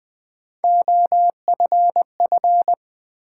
Morse code, heard as a keyed tone, about 20 wpm, 700 Hz. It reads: OFF
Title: Sets of 2 Words - New Word Drill - 20wpm